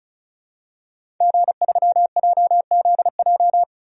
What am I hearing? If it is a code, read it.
Morse code, heard as a keyed tone, about 36 wpm, 700 Hz. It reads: G3JZJ
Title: DXCC-100 Call Signs - 5 Characters - 35wpm